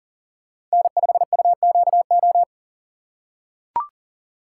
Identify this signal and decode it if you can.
Morse code, heard as a keyed tone, about 40 wpm, 700 Hz. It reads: N5UQO
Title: USA Call Signs - 5 Characters - 2x Character Spacing - 40wpm